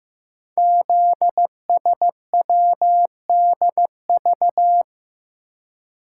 Morse code, heard as a keyed tone, about 15 wpm, 700 Hz. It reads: ZSWDV
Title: Code Groups - 5 Letters - 15wpm